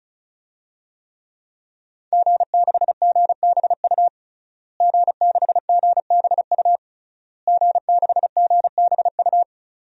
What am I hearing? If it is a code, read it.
Morse code, heard as a keyed tone, about 36 wpm, 700 Hz. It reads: G6GBU G6GBU G6GBU
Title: DXCC-100 Call Signs - 5 Characters - Repeated 3x - 35wpm